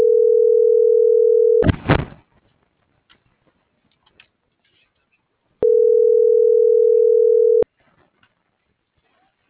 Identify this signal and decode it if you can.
Ringback tone (North America), 440 + 480 Hz.